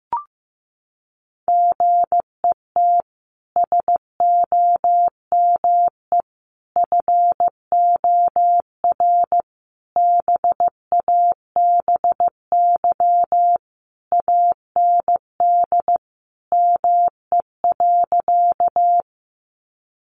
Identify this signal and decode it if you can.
Morse code, heard as a keyed tone, about 15 wpm, 700 Hz. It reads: GET SOME FOR BABY AND ME.
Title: Sentences from Top 1300 Words 15wpm